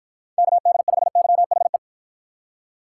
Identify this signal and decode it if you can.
Morse code, heard as a keyed tone, about 48 wpm, 700 Hz. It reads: KD5YHE